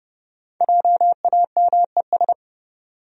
Morse code, heard as a keyed tone, about 32 wpm, 700 Hz. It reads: JAMES
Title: USA Common Names - Max 5 Letters - 30wpm